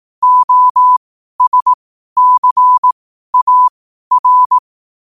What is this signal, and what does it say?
Morse code, heard as a keyed tone, about 18 wpm, spaced as for 12 wpm, 1000 Hz. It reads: OSCAR
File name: 12wpm